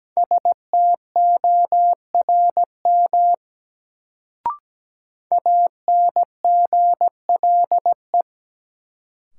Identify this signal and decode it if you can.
Morse code, heard as a keyed tone, about 17 wpm, 700 Hz. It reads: STORM ANGLE
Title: Sentences from Top 1500 Words - New Word Drill - 17wpm